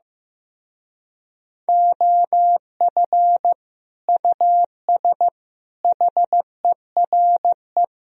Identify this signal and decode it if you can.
Morse code, heard as a keyed tone, about 15 wpm, 700 Hz. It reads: OF US HERE
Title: Speed-Racing - Sets of 5 Words 15wpm